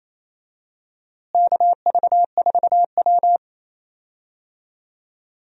Morse code, heard as a keyed tone, about 28 wpm, 700 Hz. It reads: KV4W
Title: USA Call Signs - 4 Characters - Repeated 3x - 28wpm